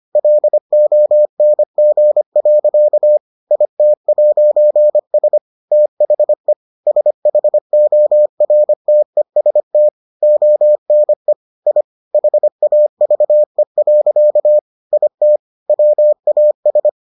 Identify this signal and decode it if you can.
Morse code, heard as a keyed tone, about 25 wpm, 600 Hz. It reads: LONG. IT'S THE SHORTEST ONE I HAVE. IT WAS